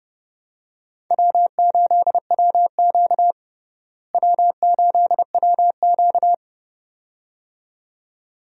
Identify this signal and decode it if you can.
Morse code, heard as a keyed tone, about 32 wpm, 700 Hz. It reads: W8WQ W8WQ
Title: USA Call Signs - 4 Characters - Repeated 2x - 30wpm